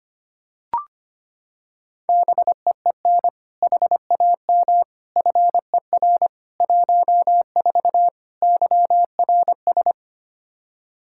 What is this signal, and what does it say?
Morse code, heard as a keyed tone, about 25 wpm, 700 Hz. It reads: BEEN HAM FER 14 YRS
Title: Practice QSO - Exchange 3 - More Info - 25wpm